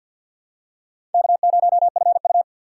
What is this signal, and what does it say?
Morse code, heard as a keyed tone, about 48 wpm, 700 Hz. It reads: K0FU
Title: USA Call Signs - 3x Character Spacing - 50wpm